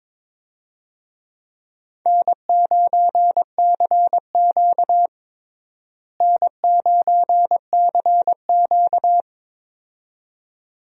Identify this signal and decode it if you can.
Morse code, heard as a keyed tone, about 23 wpm, 700 Hz. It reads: N9CQ N9CQ
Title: USA Call Signs - 4 Characters - Repeated 2x - 22wpm